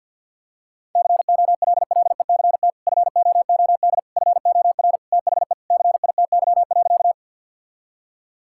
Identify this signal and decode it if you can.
Morse code, heard as a keyed tone, about 50 wpm, 700 Hz. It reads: COLLECT FOOD FOR THE CITY.
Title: Sentences from Top 1600 Words 50wpm